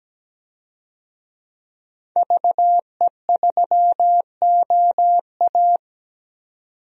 Morse code, heard as a keyed tone, about 17 wpm, 700 Hz. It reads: VE3OA
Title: DXCC-100 Call Signs - 5 Characters - 17wpm